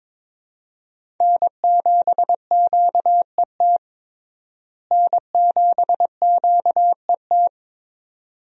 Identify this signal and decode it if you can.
Morse code, heard as a keyed tone, about 22 wpm, 700 Hz. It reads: N7QET N7QET
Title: USA Call Signs - 5 Characters - Repeated 2x - 22wpm